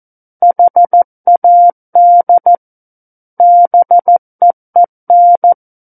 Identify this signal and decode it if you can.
Morse code, heard as a keyed tone, about 14 wpm, 700 Hz. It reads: HAD BEEN
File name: Book One / 05-Chapter 5 - 14 WPM